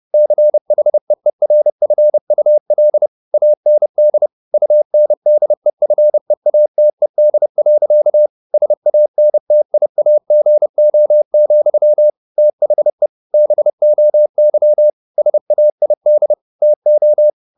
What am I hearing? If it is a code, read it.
Morse code, heard as a keyed tone, about 30 wpm, 600 Hz. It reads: CHEERFUL AND UNDEFEATED. SANTIAGO, THE BOY SAID TO
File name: the old man and the sea / the old man and the sea 1 30wpm